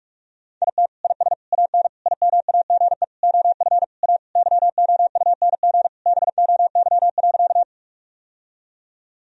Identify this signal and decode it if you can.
Morse code, heard as a keyed tone, about 46 wpm, 700 Hz. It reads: IT IS AN IMAGE OF A YOUNG BOY.
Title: Sentences from Top 1000 Words - Review Part 4 - 45wpm